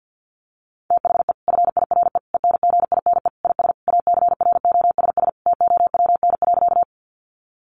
Morse code, heard as a keyed tone, about 50 wpm, 700 Hz. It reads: THE FIRE ENGINE IS ACROSS TOWN.
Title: Sentences from Top 1500 Words 50wpm